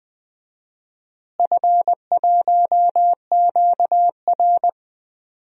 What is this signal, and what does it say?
Morse code, heard as a keyed tone, about 20 wpm, 700 Hz. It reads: F1QR
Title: DXCC-100 Call Signs - 4 Characters - Repeated 3x - 20wpm